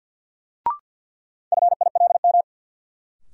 Morse code, heard as a keyed tone, about 53 wpm, 700 Hz. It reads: FILM
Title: Sentences from Top 1000 Words - Review Part 3 - New Word Drill - 50wpm